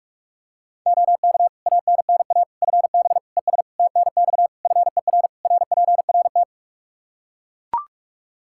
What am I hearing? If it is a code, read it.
Morse code, heard as a keyed tone, about 41 wpm, 700 Hz. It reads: OK ANNA FB ES TNX FER RPRT
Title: Practice QSO - Exchange 2 - Info - 45wpm